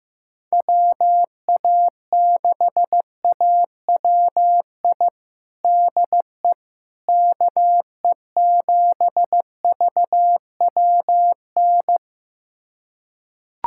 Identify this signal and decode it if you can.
Morse code, heard as a keyed tone, about 15 wpm, 700 Hz. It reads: WA6AWI DE KE7VWN
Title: USA Callsign DE Callsign - 6 Characters - 15wpm